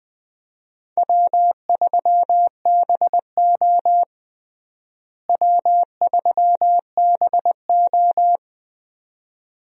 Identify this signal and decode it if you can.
Morse code, heard as a keyed tone, about 20 wpm, 700 Hz. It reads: W3BO W3BO
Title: USA Call Signs - 4 Characters - Repeated 2x - 20wpm